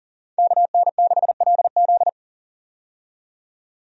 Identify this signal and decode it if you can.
Morse code, heard as a keyed tone, about 40 wpm, 700 Hz. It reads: KN6LZ
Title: USA Call Signs - 5 Characters - 40wpm